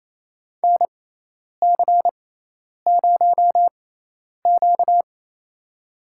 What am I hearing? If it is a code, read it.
Morse code, heard as a keyed tone, about 28 wpm, spaced as for 10 wpm, 700 Hz. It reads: NC0Q
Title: USA Call Signs - 4 Characters - 6x Character Spacing - 28wpm